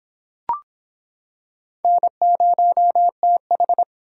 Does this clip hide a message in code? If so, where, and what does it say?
Yes: Morse code from 1.8 to 3.8 s, heard as a keyed tone, about 26 wpm, 700 Hz. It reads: N0TH